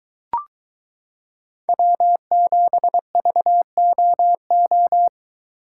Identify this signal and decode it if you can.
Morse code, heard as a keyed tone, about 23 wpm, 700 Hz. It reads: W7VOO